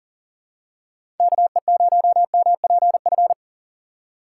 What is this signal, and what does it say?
Morse code, heard as a keyed tone, about 40 wpm, 700 Hz. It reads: KE0MPF
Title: USA Call Signs - 6 Characters - 4x Character Spacing - 40wpm